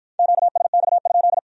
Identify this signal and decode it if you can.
Morse code, heard as a keyed tone, about 51 wpm, 700 Hz. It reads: QSX?